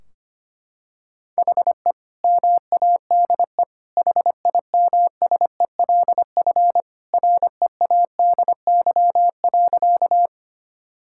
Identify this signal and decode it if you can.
Morse code, heard as a keyed tone, about 25 wpm, 700 Hz. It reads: HE MADE HIMSELF READY.
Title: Sentences from Top 1600 Words 25wpm